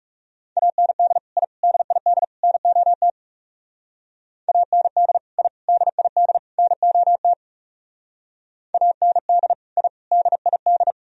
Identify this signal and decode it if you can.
Morse code, heard as a keyed tone, about 37 wpm, 700 Hz. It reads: AND I DID NOT AND I DID NOT AND I DID
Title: Speed-Racing - Sets of 4 Words 30wpm